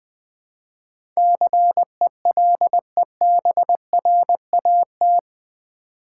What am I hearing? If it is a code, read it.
Morse code, heard as a keyed tone, about 20 wpm, 700 Hz. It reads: CELEBRAT
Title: Build A Word - 4 Letters At A Time - 11-Letter Words - 20wpm